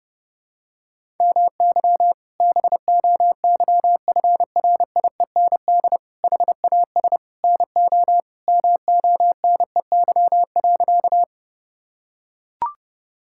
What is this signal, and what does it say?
Morse code, heard as a keyed tone, about 32 wpm, 700 Hz. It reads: MY BOYFRIEND HAS NO MONEY.
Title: Sentences from Top 1000 Words - Review Part 5 - 30wpm